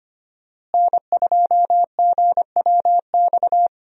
Morse code, heard as a keyed tone, about 25 wpm, 700 Hz. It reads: N2GWX
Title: USA Call Signs - 5 Characters - Repeated 2x - 25wpm